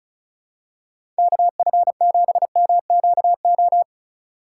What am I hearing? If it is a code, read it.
Morse code, heard as a keyed tone, about 36 wpm, 700 Hz. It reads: KF7MQO
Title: USA Call Signs - 6 Characters - 3x Character Spacing - 35wpm